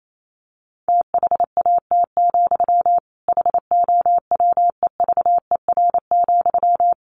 Morse code, heard as a keyed tone, about 28 wpm, 700 Hz. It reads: THAT, HOWEVER,
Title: Sentences from Top 1300 Words 28wpm